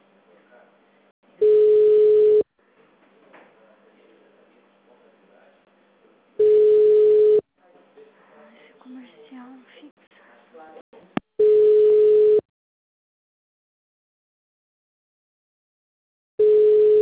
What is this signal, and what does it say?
Ringback tone (Europe, 425 Hz), with speech.